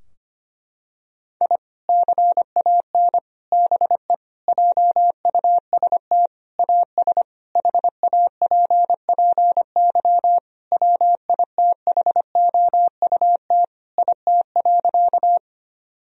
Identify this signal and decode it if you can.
Morse code, heard as a keyed tone, about 25 wpm, 700 Hz. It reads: I CAN BE JUST AS HAPPY WITHOUT IT.
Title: Sentences from Top 1000 Words - Review Part 3 - 25wpm